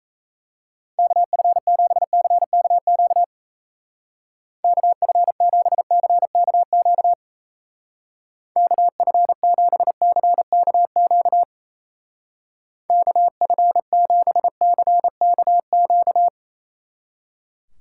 Morse code, heard as a keyed tone, about 35 wpm, 700 Hz. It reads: KF7CKQ KF7CKQ KF7CKQ KF7CKQ
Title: Speed Racing - USA 2x3 Call Signs 28wpm